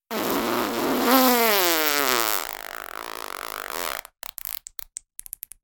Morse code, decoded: none